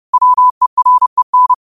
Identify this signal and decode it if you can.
Morse code, heard as a keyed tone, about 30 wpm, 1000 Hz. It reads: WEREN